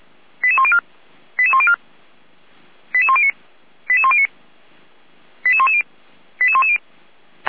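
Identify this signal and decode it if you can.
Five-tone selective call (ZVEI), heard as quick runs of five tones, tones of about 70 ms. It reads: 88104 88104 88109 88109 88110 88110
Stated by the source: ZVEI five-tone selective call received by SDR.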